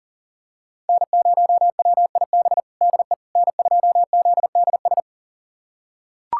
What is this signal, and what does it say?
Morse code, heard as a keyed tone, about 40 wpm, 700 Hz. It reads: N0WIB DE N2ZDS
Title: USA Callsign DE Callsign - 5 Characters - 40wpm